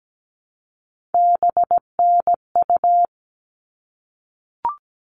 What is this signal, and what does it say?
Morse code, heard as a keyed tone, about 17 wpm, 700 Hz. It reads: BNU
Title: All Cut Numbers - 3-Digit 17wpm